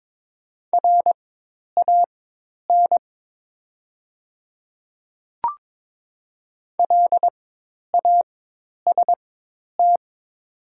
Morse code, heard as a keyed tone, about 22 wpm, spaced as for 10 wpm, 700 Hz. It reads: RAN LAST
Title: 4x Character Spacing - Top 500 Words - 22wpm